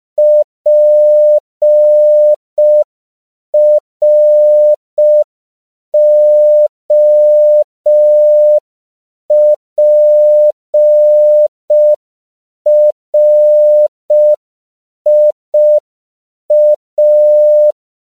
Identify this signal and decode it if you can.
Morse code, heard as a keyed tone, about 5 wpm, 600 Hz. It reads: PROPRIA